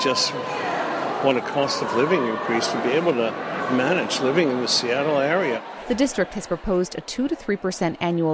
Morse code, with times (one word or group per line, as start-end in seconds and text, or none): none